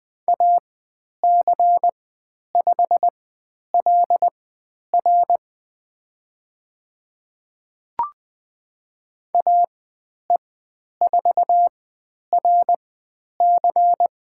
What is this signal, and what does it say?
Morse code, heard as a keyed tone, about 20 wpm, spaced as for 10 wpm, 700 Hz. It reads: AC5LR AE4RC
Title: 102 Lesson 17g - Letter u - Call Signs 20wpm